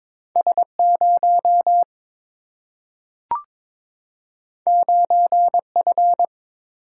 Morse code, heard as a keyed tone, about 22 wpm, 700 Hz. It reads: S 0 9 F
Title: Single-Letter Single-Number 22wpm